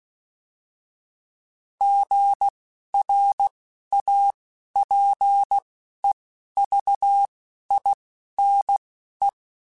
Morse code, heard as a keyed tone, about 16 wpm, spaced as for 11 wpm, 800 Hz. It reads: GRAPEVINE